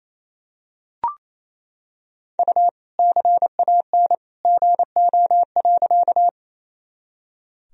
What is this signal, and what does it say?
Morse code, heard as a keyed tone, about 28 wpm, 700 Hz. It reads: U CAN GO.
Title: Sentences from Top 100 Words and Abbreviations 28wpm